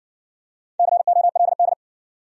Morse code, heard as a keyed tone, about 58 wpm, 700 Hz. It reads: COLD